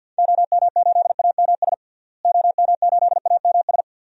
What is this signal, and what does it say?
Morse code, heard as a keyed tone, about 53 wpm, 700 Hz. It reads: OM8AMS OM8AMS
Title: DXCC-100 Call Signs - 6 Characters - Repeated 3x - 50wpm